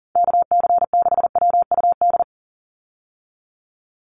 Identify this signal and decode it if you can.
Morse code, heard as a keyed tone, about 40 wpm, 690 Hz. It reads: KC6WUD
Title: USA Call Signs 40wpm